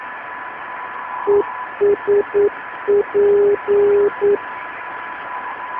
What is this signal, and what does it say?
Morse code, heard as a keyed tone, about 9 wpm, 410 Hz, with noise. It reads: ESP